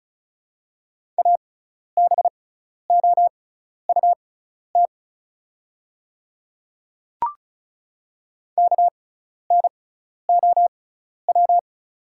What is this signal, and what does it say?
Morse code, heard as a keyed tone, about 36 wpm, spaced as for 12 wpm, 700 Hz. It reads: ABOUT KNOW
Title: 6x Character Spacing - Top 100 Words - 35wpm